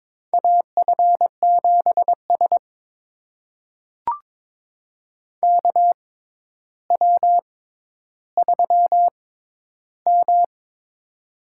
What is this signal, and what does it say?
Morse code, heard as a keyed tone, about 22 wpm, spaced as for 11 wpm, 700 Hz. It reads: AF7S KW3M
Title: USA Call Signs - 4 Characters - 6x Character Spacing - 22wpm